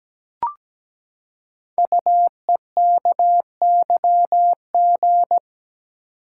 Morse code, heard as a keyed tone, about 17 wpm, 700 Hz. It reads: UEKYG
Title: Code Groups - 5 Letters - 17wpm